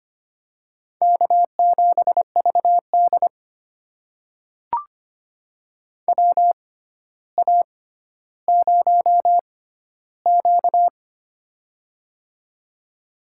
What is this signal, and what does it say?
Morse code, heard as a keyed tone, about 25 wpm, spaced as for 13 wpm, 700 Hz. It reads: K7VD WA0Q
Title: USA Call Signs - 4 Characters - 6x Character Spacing - 25wpm